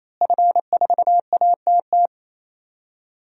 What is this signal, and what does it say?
Morse code, heard as a keyed tone, about 28 wpm, 700 Hz. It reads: F4ATT
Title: DXCC-100 Call Signs - 5 Characters - Repeated 3x - 28wpm